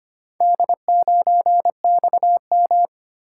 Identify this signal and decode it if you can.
Morse code, heard as a keyed tone, about 25 wpm, 700 Hz. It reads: D9XM
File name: practice / 25wpm / Four Letter-Number Combinations - 25wpm